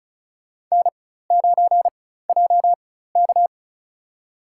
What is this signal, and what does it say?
Morse code, heard as a keyed tone, about 36 wpm, spaced as for 16 wpm, 700 Hz. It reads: N9JK
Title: USA Call Signs - 4x Character Spacing - 35wpm